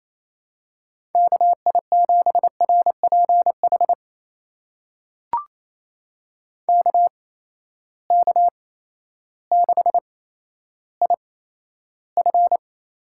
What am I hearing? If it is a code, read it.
Morse code, heard as a keyed tone, about 27 wpm, spaced as for 13 wpm, 700 Hz. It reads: KI7RPH KK6IF
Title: USA Call Signs - 6 Characters - 8x Character Spacing - 28wpm